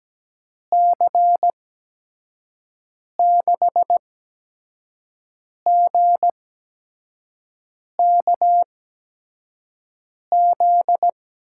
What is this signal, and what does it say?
Morse code, heard as a keyed tone, about 17 wpm, spaced as for 5 wpm, 700 Hz. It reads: C6GKZ
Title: USA Call Signs - 6 Characters - 8x Character Spacing - 17wpm